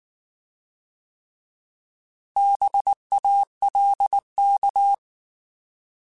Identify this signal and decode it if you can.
Morse code, heard as a keyed tone, about 19 wpm, 800 Hz. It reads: BALK
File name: Alphabetical / Alphabetical mp3s / Bs @18